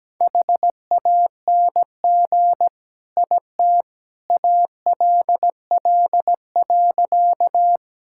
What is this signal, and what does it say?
Morse code, heard as a keyed tone, about 17 wpm, 700 Hz. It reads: HANG IT ALL.
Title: Sentences from Top 1600 Words 17wpm